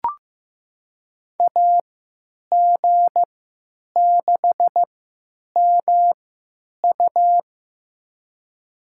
Morse code, heard as a keyed tone, about 15 wpm, spaced as for 9 wpm, 700 Hz. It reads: AG6MU
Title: USA Call Signs - 5 Characters - 3x Character Spacing - 15wpm